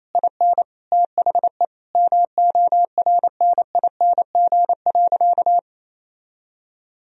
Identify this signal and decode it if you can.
Morse code, heard as a keyed tone, about 28 wpm, 700 Hz. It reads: IN THE MORNING.
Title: Sentences from Top 1000 Words - Review Part 4 - 28wpm